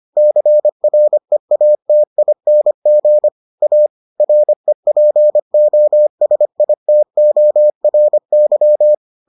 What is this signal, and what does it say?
Morse code, heard as a keyed tone, about 25 wpm, 600 Hz. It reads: CREATING A REPOSITORY